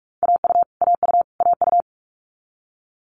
Morse code, heard as a keyed tone, about 46 wpm, 700 Hz. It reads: AU AU AU
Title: Multi-letter Phonemes Repeated 3x 45wpm